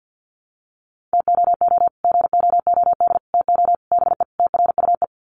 Morse code, heard as a keyed tone, about 49 wpm, 700 Hz. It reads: TOO GOOD TO BE TRUE
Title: Sets of 5 Words Encore 50wpm